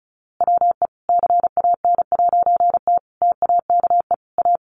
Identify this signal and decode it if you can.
Morse code, heard as a keyed tone, about 34 wpm, 700 Hz. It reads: WE CAN'T TAKE A
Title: A Sound Of Thunder 35wpm